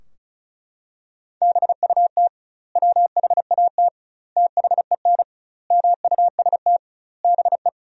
Morse code, heard as a keyed tone, about 34 wpm, 700 Hz. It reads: BUT WHAT THEN MUST BE
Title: 2x Word-Spacing - Sentences from Top 500 Words - 35wpm